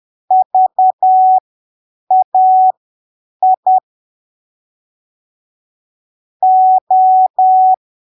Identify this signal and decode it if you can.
Morse code, heard as a keyed tone, about 10 wpm, spaced as for 7 wpm, 750 Hz. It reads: VAI O
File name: arquivos / texto / texto 13 10ppm